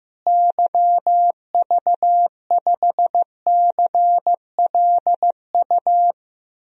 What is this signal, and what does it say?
Morse code, heard as a keyed tone, about 15 wpm, 700 Hz. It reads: YV5CLU
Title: DXCC-100 Call Signs - 6 Characters - Repeated 2x - 15wpm